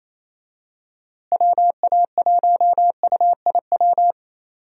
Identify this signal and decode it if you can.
Morse code, heard as a keyed tone, about 28 wpm, 700 Hz. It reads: WA1UIW
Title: USA Call Signs - 6 Characters - Repeated 3x - 28wpm